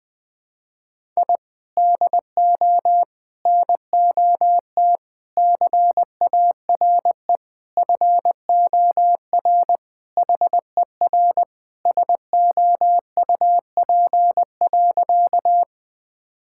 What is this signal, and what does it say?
Morse code, heard as a keyed tone, about 20 wpm, 700 Hz. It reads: I DO NOT CARE FOR HER SOUP.
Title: Sentences from Top 1000 Words - Review Part 6 - 20wpm